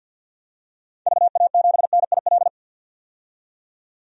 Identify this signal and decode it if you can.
Morse code, heard as a keyed tone, about 50 wpm, 700 Hz. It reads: VA7NIL
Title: DXCC-100 Call Signs - 6 Characters - 50wpm